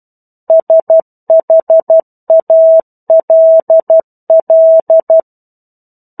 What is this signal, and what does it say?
Morse code, heard as a keyed tone, about 12 wpm, 650 Hz. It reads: SHALL
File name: 01-Chapter-01 12 WPM